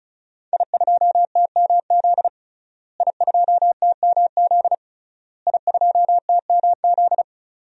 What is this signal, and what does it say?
Morse code, heard as a keyed tone, about 34 wpm, 700 Hz. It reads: I2TMZ I2TMZ I2TMZ
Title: DXCC-100 Call Signs - 5 Characters - Repeated 3x - 35wpm